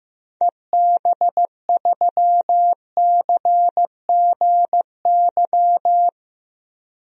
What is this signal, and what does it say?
Morse code, heard as a keyed tone, about 15 wpm, 700 Hz. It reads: EB3CGY
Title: DXCC-100 Call Signs - 6 Characters - Repeated 3x - 15wpm